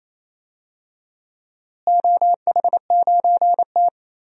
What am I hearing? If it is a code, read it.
Morse code, heard as a keyed tone, about 27 wpm, 700 Hz. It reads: OH9T
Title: DXCC-100 Call Signs - 4 Characters - Repeated 2x - 28wpm